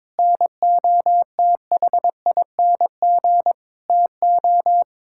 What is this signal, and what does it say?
Morse code, heard as a keyed tone, about 22 wpm, 700 Hz. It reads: NOTHING TO
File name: practice / 22wpm / Sets of 8 Words - 22wpm